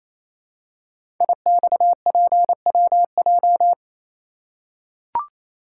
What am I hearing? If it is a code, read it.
Morse code, heard as a keyed tone, about 29 wpm, 690 Hz. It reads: IXPWJ
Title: Code Groups - 5 Letters - 28wpm